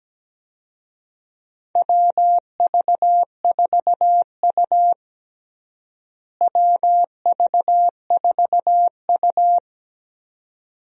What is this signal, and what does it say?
Morse code, heard as a keyed tone, about 17 wpm, 700 Hz. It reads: WV4U WV4U
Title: USA Call Signs - 4 Characters - Repeated 2x - 17wpm